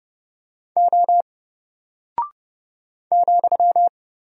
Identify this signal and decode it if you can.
Morse code, heard as a keyed tone, about 30 wpm, 700 Hz. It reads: O ,
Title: Single Character (Letter:Number:Punctuation:Prosign) Alt 30wpm